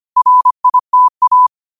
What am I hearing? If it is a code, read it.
Morse code, heard as a keyed tone, about 25 wpm, 1000 Hz. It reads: RITA
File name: qotdfiles / 25wpm